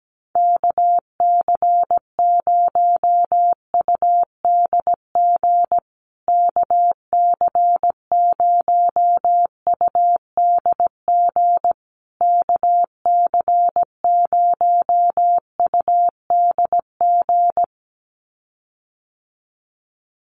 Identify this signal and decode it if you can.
Morse code, heard as a keyed tone, about 17 wpm, 700 Hz. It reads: KC0UDG KC0UDG KC0UDG
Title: USA Call Signs Repeated 3x 17wpm